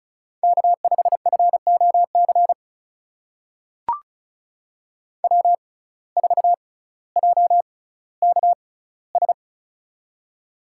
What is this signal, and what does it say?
Morse code, heard as a keyed tone, about 36 wpm, spaced as for 18 wpm, 700 Hz. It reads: K5FOC W4JKS
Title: USA Call Signs - 5 Characters - 6x Character Spacing - 35wpm